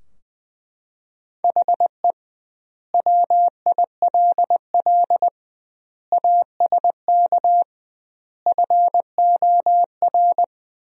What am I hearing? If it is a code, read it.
Morse code, heard as a keyed tone, about 20 wpm, 700 Hz. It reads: HE WILL ASK FOR
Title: 2x Word-Spacing - Sentences from Top 200 Words - 20wpm